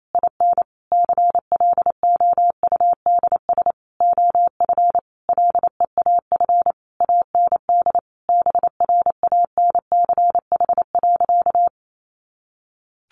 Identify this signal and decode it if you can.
Morse code, heard as a keyed tone, about 28 wpm, 700 Hz. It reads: IN CLOUDS OF LEAF AND BRANCH.
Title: A Sound Of Thunder 28wpm